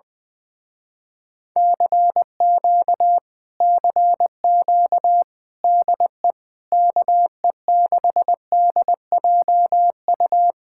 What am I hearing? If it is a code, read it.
Morse code, heard as a keyed tone, about 20 wpm, 700 Hz. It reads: CQ CQ DE KE6DJU
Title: Speed-Racing Common QSO Phrases - 17wpm